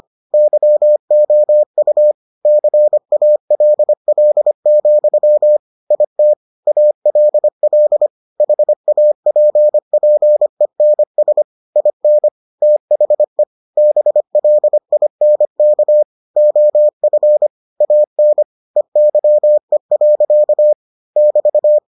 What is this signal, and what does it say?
Morse code, heard as a keyed tone, about 25 wpm, 600 Hz. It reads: YOU CALL, IT ALL HAPPENS IN THE BLINK OF AN EYE. <BT>